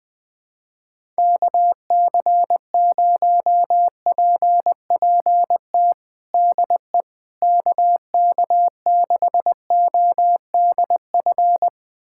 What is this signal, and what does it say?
Morse code, heard as a keyed tone, about 20 wpm, 700 Hz. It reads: KC0PPT DE KK6ODF
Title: USA Callsign DE Callsign 20wpm